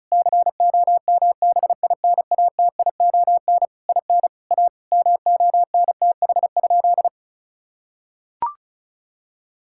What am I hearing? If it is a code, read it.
Morse code, heard as a keyed tone, about 36 wpm, 700 Hz. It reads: COMBINATION IN A MONTH?
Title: Sentences from Top 1000 Words - Review Part 6 - 35wpm